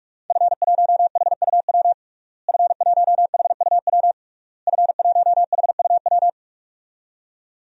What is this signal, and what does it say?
Morse code, heard as a keyed tone, about 46 wpm, 700 Hz. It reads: F1HUW F1HUW F1HUW
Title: DXCC-100 Call Signs - 5 Characters - Repeated 3x - 45wpm